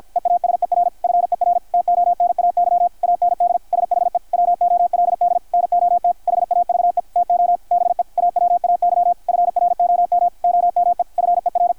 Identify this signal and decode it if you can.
Morse code, heard as a keyed tone, about 53 wpm, 700 Hz, with radio noise. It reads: EASER FER TONAY AND SHE WOLD NOT HAVE TO BE AWAY FROM OME FER
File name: archived / CRSnet